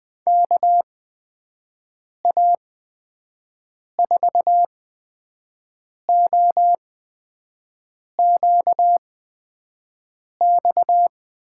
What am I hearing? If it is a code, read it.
Morse code, heard as a keyed tone, about 20 wpm, spaced as for 5 wpm, 700 Hz. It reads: KA4OQX
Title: USA Call Signs - 6 Characters - 8x Character Spacing - 20wpm